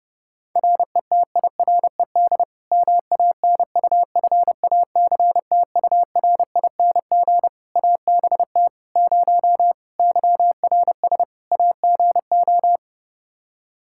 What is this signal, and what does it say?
Morse code, heard as a keyed tone, about 30 wpm, 700 Hz. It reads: RETIRED MANUFACTURING ABT 0 YRS AGO
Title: Practice QSO - Exchange 3 - More Info - 30wpm